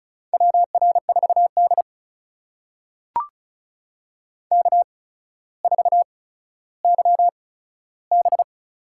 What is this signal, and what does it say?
Morse code, heard as a keyed tone, about 36 wpm, spaced as for 15 wpm, 700 Hz. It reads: WR4D K4YB
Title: USA Call Signs - 4 Characters - 8x Character Spacing - 35wpm